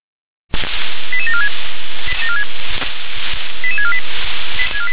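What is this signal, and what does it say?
Five-tone selective call (ZVEI), heard as quick runs of five tones, tones of about 70 ms. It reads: 88746 88648
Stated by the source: ZVEI five-tone selective call received by SDR.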